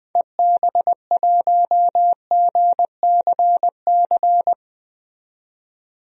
Morse code, heard as a keyed tone, about 20 wpm, 700 Hz. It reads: EB1GCC
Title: DXCC-100 Call Signs - 6 Characters - 20wpm